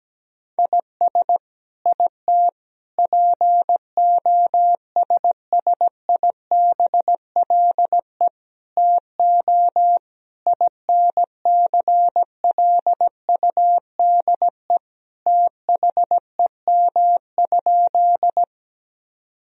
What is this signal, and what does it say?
Morse code, heard as a keyed tone, about 17 wpm, 700 Hz. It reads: IS IT POSSIBLE TO INCLUDE THEM?
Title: Sentences from Top 1000 Words - Review Part 3 - 17wpm